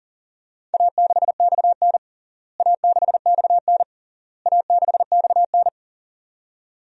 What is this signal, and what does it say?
Morse code, heard as a keyed tone, about 40 wpm, 700 Hz. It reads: A6XN A6XN A6XN
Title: DXCC-100 Call Signs - 4 Characters - Repeated 3x - 40wpm